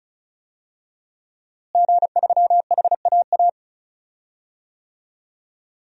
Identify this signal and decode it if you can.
Morse code, heard as a keyed tone, about 34 wpm, 700 Hz. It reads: G3HAA
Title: DXCC-100 Call Signs - 5 Characters - Repeated 2x - 35wpm